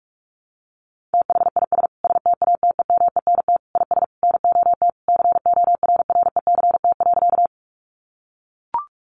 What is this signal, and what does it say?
Morse code, heard as a keyed tone, about 44 wpm, 700 Hz. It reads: THIS STATEMENT IS NOT CORRECT.
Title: Sentences from Top 1200 Words 45wpm